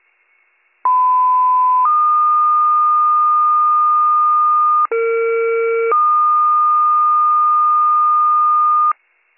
Two-tone page, heard as a one-second tone, then a three-second tone, about 1010 then 1220 Hz, about 460 then 1220 Hz. It is heard from 0.8 to 8.9 s.